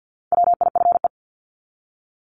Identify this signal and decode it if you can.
Morse code, heard as a keyed tone, about 50 wpm, 700 Hz. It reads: WIFE
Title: Top 1751-2000 Words 50wpm